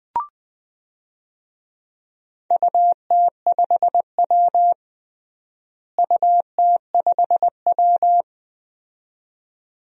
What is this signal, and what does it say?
Morse code, heard as a keyed tone, about 21 wpm, 700 Hz. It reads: UT5W UT5W
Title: DXCC-100 Call Signs - 4 Characters - Repeated 2x - 20wpm